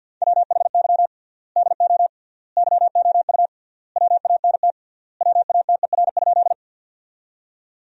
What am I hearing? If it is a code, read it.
Morse code, heard as a keyed tone, about 46 wpm, 700 Hz. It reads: WHY DO YOU WANT WATER?
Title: 3x Word-Spacing - Sentences from Top 200 Words - 50wpm